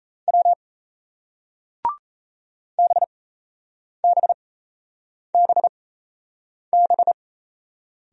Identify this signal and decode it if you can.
Morse code, heard as a keyed tone, about 37 wpm, spaced as for 8 wpm, 700 Hz. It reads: W BBBB